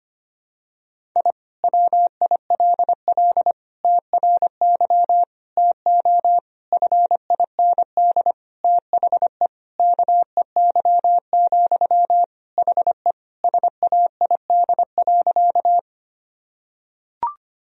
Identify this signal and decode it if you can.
Morse code, heard as a keyed tone, about 25 wpm, 700 Hz. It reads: I WILL TRY TO FIND THE KEY, HE SAID.
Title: Sentences from Top 1000 Words - Review Part 4 - 25wpm